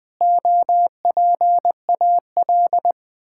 Morse code, heard as a keyed tone, about 20 wpm, 700 Hz. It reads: OPAL
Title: Pangram Challenge v2 - 20wpm